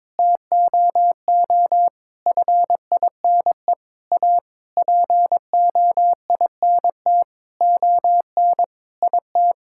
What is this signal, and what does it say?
Morse code, heard as a keyed tone, about 22 wpm, 700 Hz. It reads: TOO FINE A POINT ON IT
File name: practice / 22wpm / Sentences with 8-Word Phrases - 22wpm